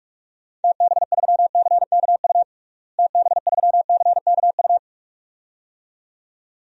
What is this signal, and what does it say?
Morse code, heard as a keyed tone, about 46 wpm, 700 Hz. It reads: TB3CKU TB3CKU
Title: DXCC-100 Call Signs - 6 Characters - Repeated 2x - 45wpm